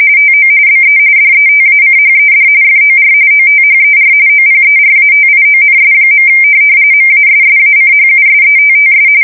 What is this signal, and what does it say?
RTTY, about 45.45 baud. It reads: TO BE DETACHED FOR OTHER OPERATIONS. MASSACHUSETTS
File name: fsk170 / USS-Massachusetts